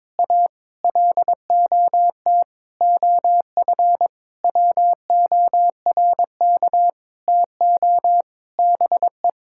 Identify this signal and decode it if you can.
Morse code, heard as a keyed tone, about 22 wpm, 700 Hz. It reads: A LOT OF WORK TO BE
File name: Sentences with 9-Word Phrases - 22wpm